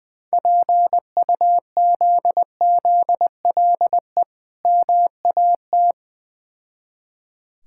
Morse code, heard as a keyed tone, about 20 wpm, 690 Hz. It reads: PUZZLE MAT
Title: Difficult Words with Hints - 20wpm